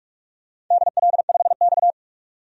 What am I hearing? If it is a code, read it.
Morse code, heard as a keyed tone, about 46 wpm, 690 Hz. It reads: DL5X
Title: DXCC-100 Call Signs - 4 Characters - 45wpm